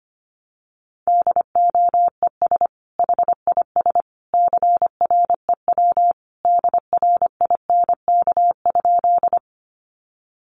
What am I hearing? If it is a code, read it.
Morse code, heard as a keyed tone, about 25 wpm, 700 Hz. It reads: DOES HIS CREW DRINK?
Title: Sentences from Top 1500 Words 25wpm